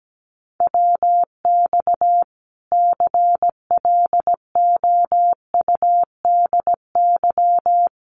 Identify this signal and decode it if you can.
Morse code, heard as a keyed tone, about 17 wpm, 700 Hz. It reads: WX CLOUDY
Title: Common QSO Phrases 17wpm